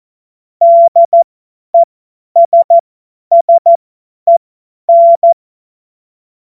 Morse code, heard as a keyed tone, about 14 wpm, spaced as for 10 wpm, 680 Hz. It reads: DESSEN
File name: Morsekurs / Witze / Witz 01. März 2023 50 F70